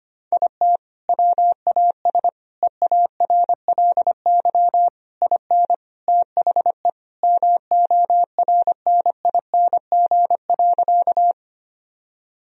Morse code, heard as a keyed tone, about 25 wpm, 700 Hz. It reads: IT WAS EARLY IN THE MORNING.
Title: Sentences from Top 1000 Words - Review Part 2 - 25wpm